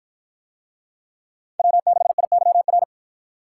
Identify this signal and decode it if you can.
Morse code, heard as a keyed tone, about 53 wpm, 700 Hz. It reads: W6IYR